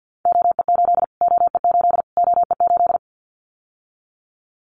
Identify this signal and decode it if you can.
Morse code, heard as a keyed tone, about 50 wpm, 700 Hz. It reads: OE8 OE8 OE8
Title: Callsign Prefixes for Top 25 DXCC - Repeated 3x - 50wpm